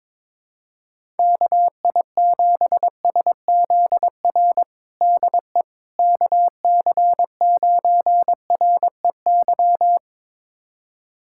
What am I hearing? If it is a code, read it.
Morse code, heard as a keyed tone, about 21 wpm, 700 Hz. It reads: KI7SZR DE KC9REY
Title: USA Callsign DE Callsign - 6 Characters - 22wpm